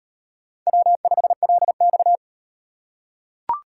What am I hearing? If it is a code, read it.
Morse code, heard as a keyed tone, about 38 wpm, 700 Hz. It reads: W5LX